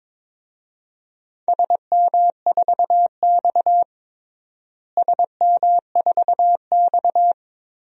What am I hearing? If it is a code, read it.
Morse code, heard as a keyed tone, about 22 wpm, 700 Hz. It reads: SM4X SM4X
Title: DXCC-100 Call Signs - 4 Characters - Repeated 3x - 22wpm